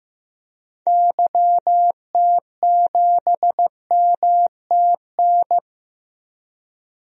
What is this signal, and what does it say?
Morse code, heard as a keyed tone, about 15 wpm, 700 Hz. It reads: YT7MTN
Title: DXCC-100 Call Signs - 6 Characters - 15wpm